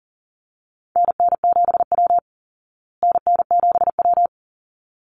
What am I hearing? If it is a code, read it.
Morse code, heard as a keyed tone, about 40 wpm, 700 Hz. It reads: NN7W NN7W
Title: Call Signs Repeated 3x - 40wpm